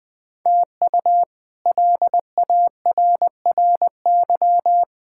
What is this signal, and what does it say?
Morse code, heard as a keyed tone, about 20 wpm, 700 Hz. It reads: TU LARRY
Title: K1USN SST QSO Practice - 20wpm